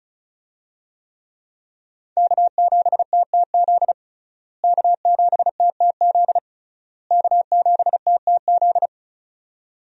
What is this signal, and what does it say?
Morse code, heard as a keyed tone, about 36 wpm, 700 Hz. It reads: K7TTZ K7TTZ K7TTZ
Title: USA Call Signs - 5 Characters - Repeated 3x - 35wpm